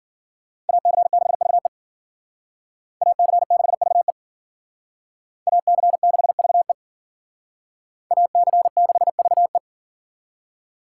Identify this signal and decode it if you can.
Morse code, heard as a keyed tone, about 52 wpm, 700 Hz. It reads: AC6VE AC6VE AC6VE AC6VE